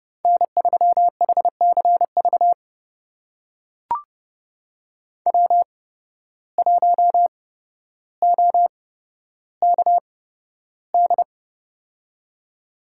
Morse code, heard as a keyed tone, about 28 wpm, spaced as for 13 wpm, 700 Hz. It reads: N3HCV W1OKD
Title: USA Call Signs - 5 Characters - 8x Character Spacing - 30wpm